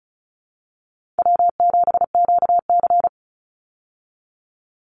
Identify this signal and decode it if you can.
Morse code, heard as a keyed tone, about 34 wpm, 700 Hz. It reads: W7QC
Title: Call Signs Repeated 3x - 35wpm